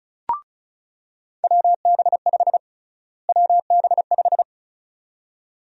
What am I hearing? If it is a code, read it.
Morse code, heard as a keyed tone, about 36 wpm, 700 Hz. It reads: WB5 WB5
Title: USA Call Sign Prefixes - Repeated 2x - 35wpm